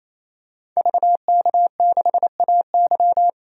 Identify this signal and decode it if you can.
Morse code, heard as a keyed tone, about 28 wpm, 700 Hz. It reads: VK6AY
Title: DXCC-100 Call Signs - 5 Characters - 28wpm